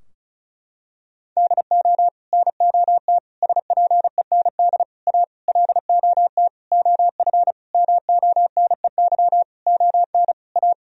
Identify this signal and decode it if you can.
Morse code, heard as a keyed tone, about 34 wpm, 700 Hz. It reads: DO NOT SPEND A LOT OF MONEY ON A
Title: Sentences from Top 1000 Words - Review Part 4 - 35wpm